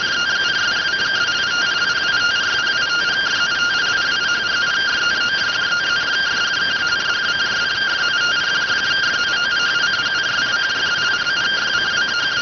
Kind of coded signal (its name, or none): two-tone teleprinter signal